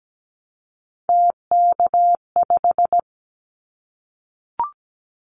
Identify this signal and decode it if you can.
Morse code, heard as a keyed tone, about 17 wpm, 700 Hz. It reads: TK5
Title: Callsign Prefixes for Top 100 DXCC - 17wpm